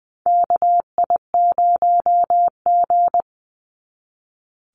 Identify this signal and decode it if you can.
Morse code, heard as a keyed tone, about 20 wpm, 690 Hz. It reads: KI0G
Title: NA SOTA Chaser Callsigns 20wpm